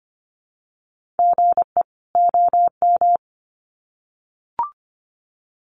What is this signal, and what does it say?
Morse code, heard as a keyed tone, about 25 wpm, 700 Hz. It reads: GE OM
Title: Common QSO Phrases 25wpm